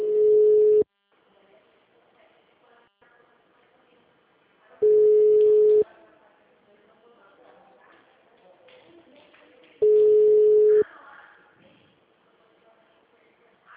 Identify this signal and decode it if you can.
Ringback tone (Europe, 425 Hz), with speech.
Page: Dracena (Ura-Reversa)